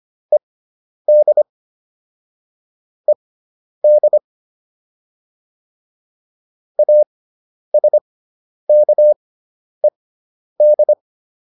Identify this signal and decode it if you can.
Morse code, heard as a keyed tone, about 25 wpm, spaced as for 10 wpm, 600 Hz. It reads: ED ED ASKED
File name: ED5 10